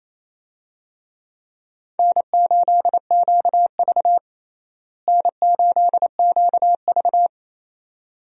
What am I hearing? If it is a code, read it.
Morse code, heard as a keyed tone, about 28 wpm, 700 Hz. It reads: N8QV N8QV
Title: USA Call Signs - 4 Characters - Repeated 2x - 28wpm